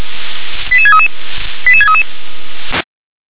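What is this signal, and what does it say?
Five-tone selective call (ZVEI), heard as quick runs of five tones, tones of about 70 ms. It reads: 88633 88633
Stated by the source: ZVEI five-tone call received by SDR.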